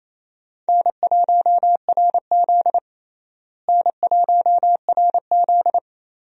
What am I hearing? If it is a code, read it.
Morse code, heard as a keyed tone, about 28 wpm, 700 Hz. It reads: N1RZ N1RZ
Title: USA Call Signs - 4 Characters - Repeated 3x - 28wpm